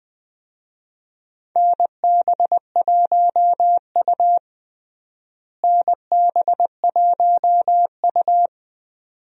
Morse code, heard as a keyed tone, about 20 wpm, 700 Hz. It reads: NB1U NB1U
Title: USA Call Signs - 4 Characters - Repeated 2x - 20wpm